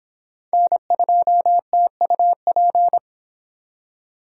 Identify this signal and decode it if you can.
Morse code, heard as a keyed tone, about 28 wpm, 700 Hz. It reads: N2TUP